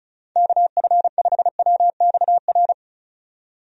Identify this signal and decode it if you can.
Morse code, heard as a keyed tone, about 34 wpm, 700 Hz. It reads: KF5WXR
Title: USA Call Signs - 6 Characters - 35wpm